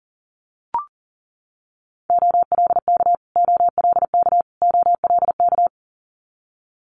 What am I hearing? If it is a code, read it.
Morse code, heard as a keyed tone, about 40 wpm, 700 Hz. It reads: OLK OLK OLK
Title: Multi-letter Phonemes Repeated 3x 40wpm